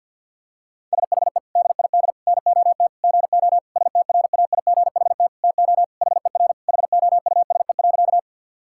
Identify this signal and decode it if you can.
Morse code, heard as a keyed tone, about 50 wpm, 700 Hz. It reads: SHE DID NOT GO STRAIGHT TO HER HOUSE.
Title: Sentences from Top 1000 Words - Review Part 6 - 50wpm